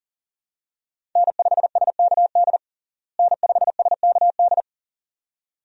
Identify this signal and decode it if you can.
Morse code, heard as a keyed tone, about 40 wpm, 700 Hz. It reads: N5SKD N5SKD
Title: USA Call Signs - 5 Characters - Repeated 2x - 40wpm